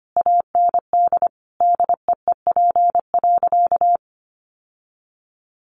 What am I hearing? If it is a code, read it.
Morse code, heard as a keyed tone, about 25 wpm, 700 Hz. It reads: AND DEEP.
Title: Sentences from Top 1100 Words 25wpm